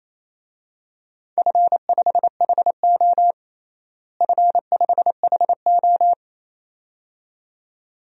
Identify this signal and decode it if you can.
Morse code, heard as a keyed tone, about 28 wpm, 700 Hz. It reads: F5HO F5HO
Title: DXCC-100 Call Signs - 4 Characters - Repeated 2x - 28wpm